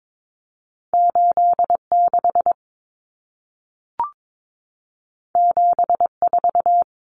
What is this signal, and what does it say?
Morse code, heard as a keyed tone, about 22 wpm, 700 Hz. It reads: 86 74
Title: Numbers - 2-Digit 22wpm